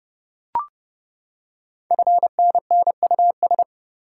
Morse code, heard as a keyed tone, about 28 wpm, 690 Hz. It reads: FNNUS
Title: Code Groups - 5 Letters - 30wpm